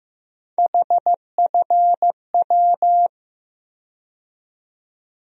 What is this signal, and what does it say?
Morse code, heard as a keyed tone, about 15 wpm, 700 Hz. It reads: HFW
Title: USA Call Sign Postfixes - 3 Letters 15wpm